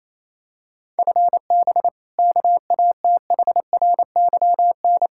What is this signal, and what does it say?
Morse code, heard as a keyed tone, about 28 wpm, 700 Hz. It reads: FB KATHRYN
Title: Practice QSO - Exchange 3 - More Info - 28wpm